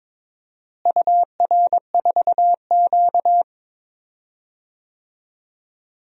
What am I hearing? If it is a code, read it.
Morse code, heard as a keyed tone, about 21 wpm, 700 Hz. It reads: UR4Q
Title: DXCC-100 Call Signs - 4 Characters - Repeated 2x - 22wpm